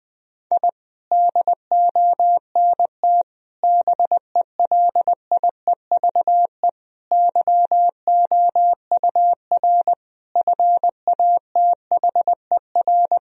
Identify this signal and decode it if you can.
Morse code, heard as a keyed tone, about 21 wpm, 700 Hz. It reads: I DONT BELIEVE YOUR FATHER
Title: Sentences from Top 1000 Words - Review Part 3 - 20wpm